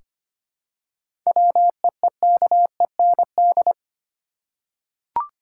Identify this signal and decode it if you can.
Morse code, heard as a keyed tone, about 25 wpm, 700 Hz. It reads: WEEKEND
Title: Seven Letter Words - 25wpm